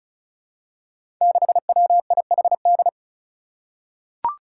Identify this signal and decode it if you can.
Morse code, heard as a keyed tone, about 36 wpm, 700 Hz. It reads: 6WIHD
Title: Five Challenging Letter-Number Combinations 35wpm